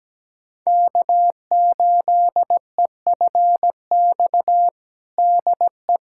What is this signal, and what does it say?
Morse code, heard as a keyed tone, about 17 wpm, 700 Hz. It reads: K8EFX DE
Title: USA Callsign DE Callsign - 5 Characters - 17wpm